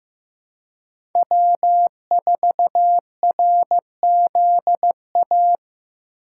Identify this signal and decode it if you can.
Morse code, heard as a keyed tone, about 15 wpm, 700 Hz. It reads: W4RZA
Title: USA Call Signs - 3x Character Spacing - 15wpm